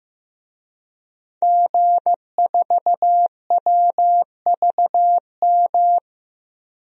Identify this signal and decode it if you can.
Morse code, heard as a keyed tone, about 15 wpm, 700 Hz. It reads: G4WVM
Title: DXCC-100 Call Signs - 5 Characters - 15wpm